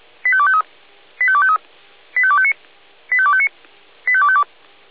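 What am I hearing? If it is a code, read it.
Five-tone selective call (ZVEI), heard as quick runs of five tones, tones of about 70 ms. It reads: 85253 85253 85259 85259 85252
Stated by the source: ZVEI five-tone selective call received on SDR.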